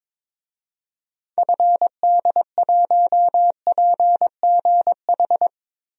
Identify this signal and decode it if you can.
Morse code, heard as a keyed tone, about 22 wpm, 700 Hz. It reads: FD1PGH
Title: DXCC-100 Call Signs - 6 Characters - 22wpm